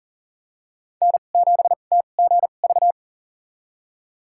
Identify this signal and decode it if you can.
Morse code, heard as a keyed tone, about 40 wpm, spaced as for 29 wpm, 700 Hz. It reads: N7TGV
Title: USA Call Signs - 2x Character Spacing - 40wpm